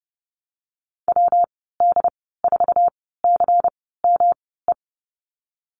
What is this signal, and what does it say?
Morse code, heard as a keyed tone, about 30 wpm, spaced as for 17 wpm, 700 Hz. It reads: WD4CME
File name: Lesson 19g - Letter w - Call Signs 30wpm